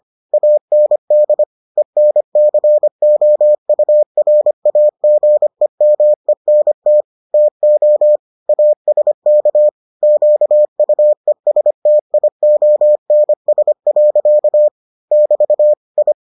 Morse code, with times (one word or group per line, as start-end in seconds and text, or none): 0.3-1.4 AND
1.8-7.0 ENCOURAGEMENT
7.3-8.2 TO
8.5-9.7 ASK
10.0-14.7 QUESTIONS.
15.1-15.7 <BT>
16.0-16.1 I